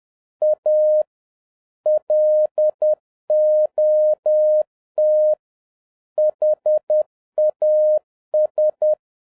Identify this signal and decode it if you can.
Morse code, heard as a keyed tone, about 10 wpm, 610 Hz. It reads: A LOT HAS